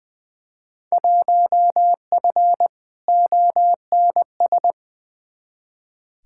Morse code, heard as a keyed tone, about 20 wpm, 700 Hz. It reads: 1F ONS
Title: Field Day Exchanges Encore - 20wpm